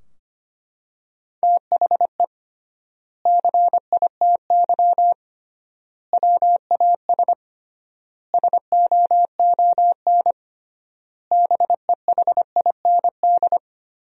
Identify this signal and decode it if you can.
Morse code, heard as a keyed tone, about 25 wpm, 700 Hz. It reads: THE CITY WAS SOON BEHIND
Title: 3x Word-Spacing - Sentences from Top 500 Words - 25wpm